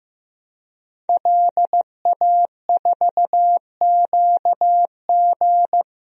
Morse code, heard as a keyed tone, about 15 wpm, 700 Hz. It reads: LA4QG
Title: DXCC-100 Call Signs - 5 Characters - Repeated 2x - 15wpm